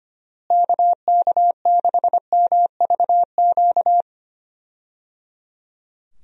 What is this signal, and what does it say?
Morse code, heard as a keyed tone, about 25 wpm, 700 Hz. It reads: KK6MVQ
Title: USA Call Signs - 6 Characters - Repeated 3x - 25wpm